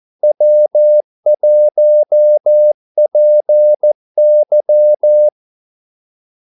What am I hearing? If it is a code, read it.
Morse code, heard as a keyed tone, about 14 wpm, 600 Hz. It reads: W1PY
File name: Callsigns-14wpm